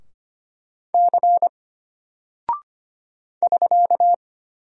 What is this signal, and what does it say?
Morse code, heard as a keyed tone, about 25 wpm, 700 Hz. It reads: C <SK>
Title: Single Character (Letter:Number:Punctuation:Prosign) Alt 25wpm